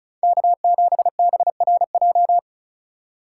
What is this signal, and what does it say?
Morse code, heard as a keyed tone, about 34 wpm, 700 Hz. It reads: K7BRJ
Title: USA Call Signs - 5 Characters - 4x Character Spacing - 35wpm